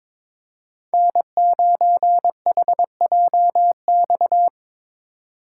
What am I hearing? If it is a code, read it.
Morse code, heard as a keyed tone, about 26 wpm, 700 Hz. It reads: N9HJX
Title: Speed Racing - USA Call Signs 22wpm